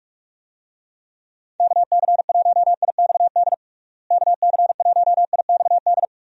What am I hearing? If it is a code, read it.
Morse code, heard as a keyed tone, about 46 wpm, 700 Hz. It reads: KC1IXD KC1IXD
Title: USA Call Signs - 6 Characters - Repeated 3x - 45wpm